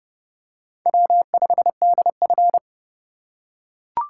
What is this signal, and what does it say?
Morse code, heard as a keyed tone, about 30 wpm, 700 Hz. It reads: W5DF
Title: USA Call Signs - 4 Characters - 4x Character Spacing - 30wpm